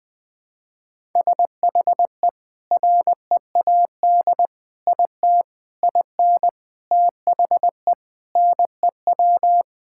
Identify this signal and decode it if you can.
Morse code, heard as a keyed tone, about 20 wpm, 700 Hz. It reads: SHE READ IT IN THE NEW
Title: Sentences with 5-Word Phrases - Spoken Beginning - 20wpm